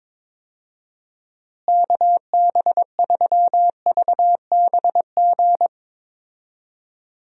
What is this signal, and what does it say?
Morse code, heard as a keyed tone, about 22 wpm, 700 Hz. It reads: KB3VBG
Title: USA Call Signs - 6 Characters - Repeated 3x - 22wpm